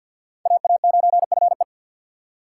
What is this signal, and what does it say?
Morse code, heard as a keyed tone, about 61 wpm, 700 Hz. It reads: AA9FE
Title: Speed Racing - USA 2x2 Call Signs 50wpm